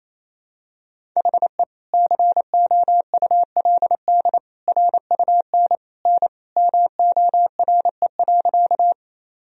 Morse code, heard as a keyed tone, about 28 wpm, 700 Hz. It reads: HE COULD RUN N MORE.
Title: Sentences from Top 300 Words and Abbreviations 28wpm